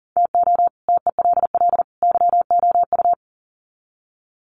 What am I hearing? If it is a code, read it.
Morse code, heard as a keyed tone, about 40 wpm, 700 Hz. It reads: TO TELL YOU
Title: Sets of 3 Words 40wpm